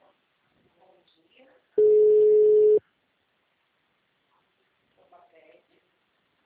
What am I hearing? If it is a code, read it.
Ringback tone (Europe, 425 Hz), with speech.
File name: out- -2050- - - .25871